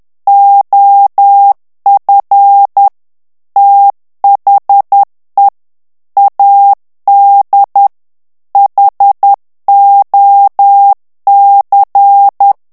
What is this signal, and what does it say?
Morse code, heard as a keyed tone, about 10 wpm, 800 Hz. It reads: OF THE AD HOC
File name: ANGLAIS texte clair 3 vitesse 10 mots minute